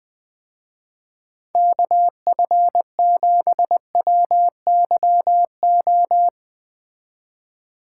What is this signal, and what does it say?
Morse code, heard as a keyed tone, about 20 wpm, 700 Hz. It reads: KF7WYO
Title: Speed Racing - USA 2x3 Call Signs 17wpm